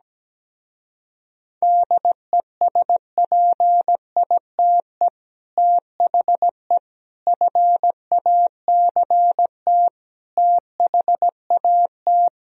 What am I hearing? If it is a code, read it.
Morse code, heard as a keyed tone, about 17 wpm, 700 Hz. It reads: DESPITE THE FACT THAT
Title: Speed-Racing - Sets of 4 Words 17wpm